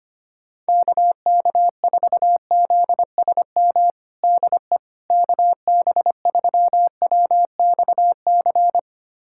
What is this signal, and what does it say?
Morse code, heard as a keyed tone, about 25 wpm, 700 Hz. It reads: KK4ZSM DE KB3WXC
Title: USA Callsign DE Callsign - 6 Characters - 25wpm